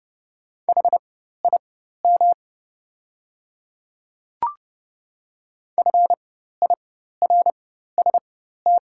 Morse code, heard as a keyed tone, about 30 wpm, spaced as for 14 wpm, 700 Hz. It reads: HIM FIRST
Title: 4x Character Spacing - Top 100 Words - 30wpm